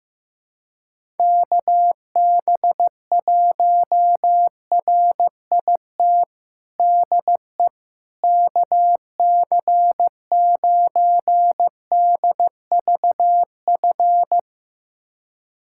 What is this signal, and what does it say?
Morse code, heard as a keyed tone, about 15 wpm, 700 Hz. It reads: KB1RIT DE KC9DVF
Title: USA Callsign DE Callsign - 6 Characters - 15wpm